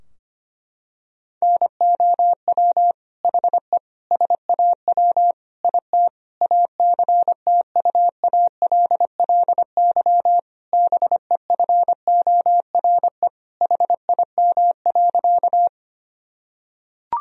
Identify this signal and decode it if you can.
Morse code, heard as a keyed tone, about 25 wpm, 700 Hz. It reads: NOW HE SAW IT ACTUALLY BEFORE HIM.
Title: Sentences from Top 1000 Words - Review Part 4 - 25wpm